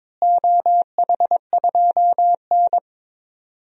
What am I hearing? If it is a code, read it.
Morse code, heard as a keyed tone, about 22 wpm, 700 Hz. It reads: OH2N
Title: DXCC-100 Call Signs - 4 Characters - Repeated 2x - 22wpm